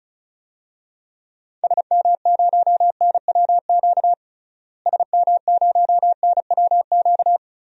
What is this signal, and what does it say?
Morse code, heard as a keyed tone, about 36 wpm, 700 Hz. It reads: SM0NWQ SM0NWQ
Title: DXCC-100 Call Signs - 6 Characters - Repeated 3x - 35wpm